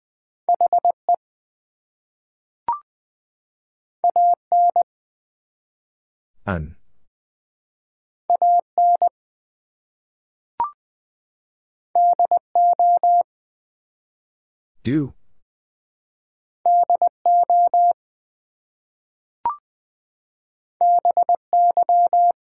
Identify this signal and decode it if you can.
Morse code, heard as a keyed tone, about 20 wpm, 700 Hz. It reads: HE AN AN DO DO BY
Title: Two Letter Words - 20wpm